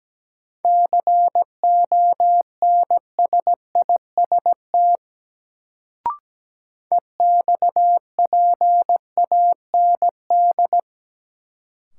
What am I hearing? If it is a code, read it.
Morse code, heard as a keyed tone, about 17 wpm, 700 Hz. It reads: CONSIST EXPAND
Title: Sentences from Top 1400 Words - New Word Drill Alt - 17wpm